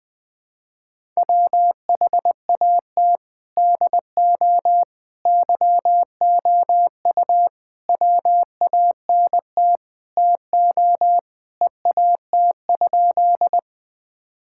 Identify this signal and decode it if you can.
Morse code, heard as a keyed tone, about 20 wpm, 700 Hz. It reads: WHAT DO YOU WANT TO EAT?
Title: Sentences with 5-Word Phrases - 20wpm